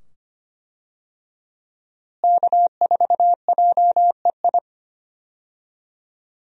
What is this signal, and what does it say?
Morse code, heard as a keyed tone, about 25 wpm, 700 Hz. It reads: K4JEI
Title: USA Call Signs - 5 Characters - Repeated 2x - 25wpm